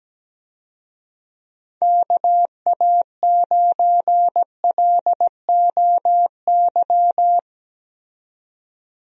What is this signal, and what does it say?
Morse code, heard as a keyed tone, about 17 wpm, 700 Hz. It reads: KA9LOY
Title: USA Call Signs - 6 Characters - 17wpm